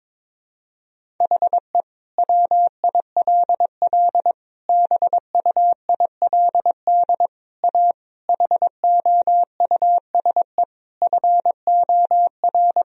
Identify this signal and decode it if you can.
Morse code, heard as a keyed tone, about 22 wpm, 700 Hz. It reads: HE WILL BUILD A HOUSE FOR
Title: Sentences from Top 1000 Words - Review Part 1 - 22wpm